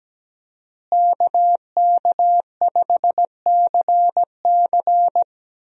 Morse code, heard as a keyed tone, about 17 wpm, 700 Hz. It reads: KK5CC
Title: USA Call Signs - 5 Characters - Repeated 3x - 17wpm